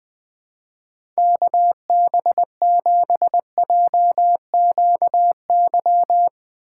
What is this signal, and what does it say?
Morse code, heard as a keyed tone, about 22 wpm, 700 Hz. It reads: KB7JQY
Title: Speed Racing - USA 2x3 Call Signs 20wpm